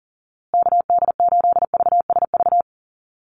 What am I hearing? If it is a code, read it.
Morse code, heard as a keyed tone, about 39 wpm, 700 Hz. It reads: KD8VSV
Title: USA Call Signs Repeated 3x 40wpm